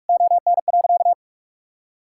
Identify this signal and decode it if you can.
Morse code, heard as a keyed tone, about 41 wpm, 700 Hz. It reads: ON.
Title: Sentences from Top 1000 Words - Review Part 6 - 45wpm